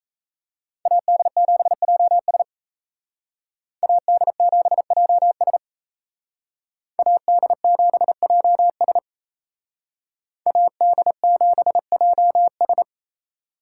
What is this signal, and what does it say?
Morse code, heard as a keyed tone, about 36 wpm, 700 Hz. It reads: AD7JS AD7JS AD7JS AD7JS